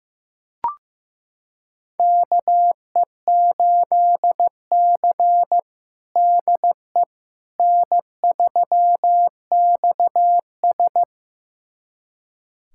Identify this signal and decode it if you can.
Morse code, heard as a keyed tone, about 16 wpm, 700 Hz. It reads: KE8C DE N3XS
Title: USA Callsign DE Callsign - 4 Characters - 15wpm